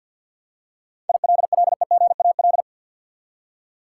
Morse code, heard as a keyed tone, about 52 wpm, 700 Hz. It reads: ILLEGAL
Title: Seven Letter Words - 50wpm